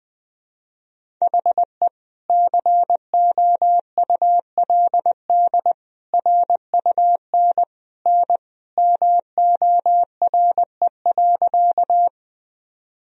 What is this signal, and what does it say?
Morse code, heard as a keyed tone, about 20 wpm, 700 Hz. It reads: HE COULD RUN N MORE.
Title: Sentences from Top 300 Words and Abbreviations 20wpm